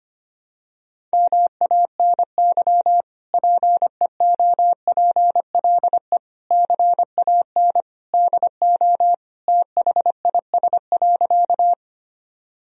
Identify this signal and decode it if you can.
Morse code, heard as a keyed tone, about 25 wpm, 700 Hz. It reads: MANY PEOPLE CAN DO THIS.
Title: Sentences from Top 1000 Words - Review Part 1 - 25wpm